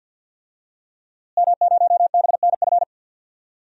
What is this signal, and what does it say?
Morse code, heard as a keyed tone, about 49 wpm, 700 Hz. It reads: M0BNF
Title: DXCC-100 Call Signs - 5 Characters - Repeated 2x - 50wpm